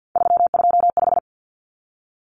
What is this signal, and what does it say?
Morse code, heard as a keyed tone, about 50 wpm, 700 Hz. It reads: 325
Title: Numbers - 3-Digit 50wpm